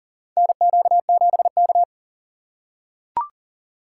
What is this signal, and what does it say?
Morse code, heard as a keyed tone, about 40 wpm, 700 Hz. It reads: NQ7K
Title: USA Call Signs - 4 Characters - 4x Character Spacing - 40wpm